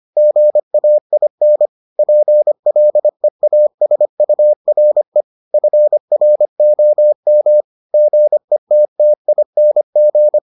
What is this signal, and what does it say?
Morse code, heard as a keyed tone, about 25 wpm, 600 Hz. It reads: GAIN PLEASURE FROM GETTING